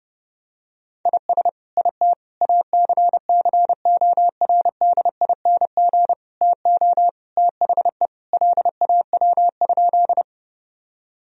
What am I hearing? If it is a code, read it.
Morse code, heard as a keyed tone, about 30 wpm, 700 Hz. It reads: IS IT ACCORDING TO THE LAW?
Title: Sentences from Top 1000 Words - Review Part 4 - 30wpm